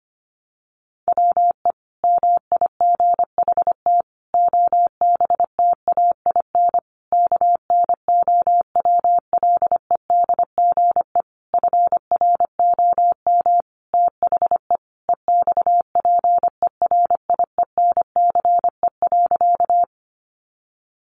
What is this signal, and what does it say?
Morse code, heard as a keyed tone, about 25 wpm, 700 Hz. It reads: WE MIGHT OBTAIN KNOWLEDGE FROM THE EXPERIENCE.
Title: Sentences from Top 1300 Words 25wpm